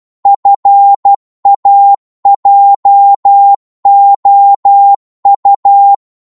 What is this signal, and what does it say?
Morse code, heard as a keyed tone, about 12 wpm, 800 Hz. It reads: FAJOU
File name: Morse / ltr-12wpm